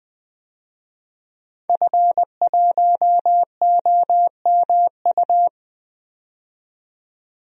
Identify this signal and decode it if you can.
Morse code, heard as a keyed tone, about 20 wpm, 700 Hz. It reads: F1OMU
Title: DXCC-100 Call Signs - 5 Characters - 20wpm